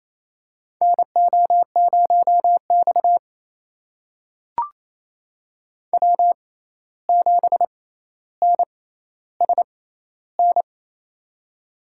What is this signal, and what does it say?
Morse code, heard as a keyed tone, about 28 wpm, spaced as for 13 wpm, 700 Hz. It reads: NO0X W7NSN
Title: USA Call Signs - 6x Character Spacing - 28wpm